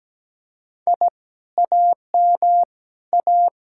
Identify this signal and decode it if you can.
Morse code, heard as a keyed tone, about 18 wpm, 700 Hz. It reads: I AM A
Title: Speed-Racing Common QSO Phrases Encore - 17wpm